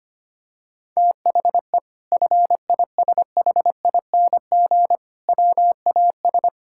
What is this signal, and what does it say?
Morse code, heard as a keyed tone, about 25 wpm, 700 Hz. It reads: THE FISHING WAS
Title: Sentences from Top 1000 Words - Review Part 5 - 25wpm